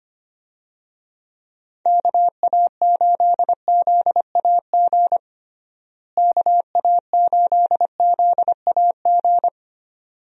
Morse code, heard as a keyed tone, about 25 wpm, 700 Hz. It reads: KA8ZAG KA8ZAG
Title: USA Call Signs - 6 Characters - Repeated 2x - 25wpm